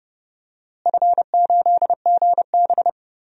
Morse code, heard as a keyed tone, about 30 wpm, 700 Hz. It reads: F8GB
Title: DXCC-100 Call Signs - 4 Characters - Repeated 3x - 30wpm